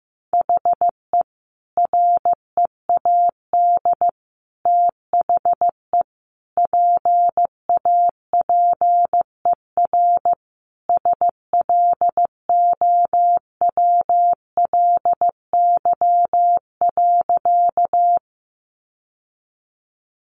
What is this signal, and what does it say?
Morse code, heard as a keyed tone, about 15 wpm, 700 Hz. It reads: HE READ THE PAPER SLOWLY.
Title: Sentences from Top 1100 Words 15wpm